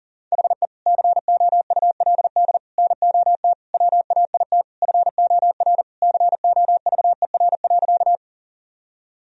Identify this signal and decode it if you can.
Morse code, heard as a keyed tone, about 37 wpm, 700 Hz. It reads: HE COULD NOT WAIT FOR COVER.
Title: Sentences from Top 1000 Words - Review Part 1 - 40wpm